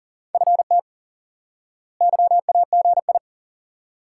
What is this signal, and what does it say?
Morse code, heard as a keyed tone, about 40 wpm, 700 Hz. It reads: FT YAGI
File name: IQE Course 3 - Lesson 89 - YAGI - Two Elements - 40wpm